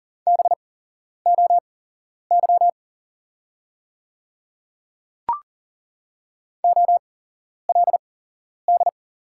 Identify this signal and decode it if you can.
Morse code, heard as a keyed tone, about 40 wpm, spaced as for 11 wpm, 700 Hz. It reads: BOY OLD
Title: 8x Character Spacing - Top 200 Words - 40wpm